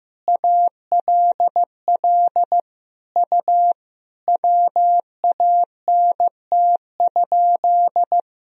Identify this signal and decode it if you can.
Morse code, heard as a keyed tone, about 15 wpm, 700 Hz. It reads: ALL U WANT?
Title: Sentences from Top 200 Words and Abbreviations 15wpm